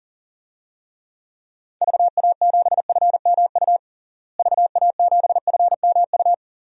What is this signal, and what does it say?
Morse code, heard as a keyed tone, about 40 wpm, 700 Hz. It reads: VA7FMU VA7FMU
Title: DXCC-100 Call Signs - 6 Characters - Repeated 2x - 40wpm